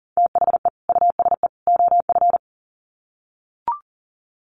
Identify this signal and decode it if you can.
Morse code, heard as a keyed tone, about 40 wpm, 700 Hz. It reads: THE USE OF
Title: Sets of 3 Words 40wpm